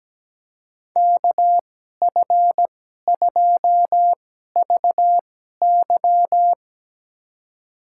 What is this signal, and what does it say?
Morse code, heard as a keyed tone, about 17 wpm, spaced as for 12 wpm, 700 Hz. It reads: KF2VY
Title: USA Call Signs - 5 Characters - 2x Character Spacing - 17wpm